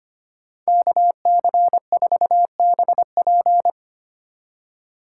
Morse code, heard as a keyed tone, about 29 wpm, 700 Hz. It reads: KC4BP
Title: Speed Racing - USA 2x2 Call Signs 25wpm